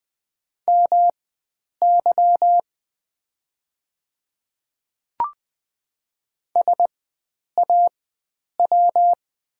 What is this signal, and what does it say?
Morse code, heard as a keyed tone, about 20 wpm, spaced as for 9 wpm, 700 Hz. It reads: MY SAW
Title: 4x Character Spacing - Top 300 Words - 20wpm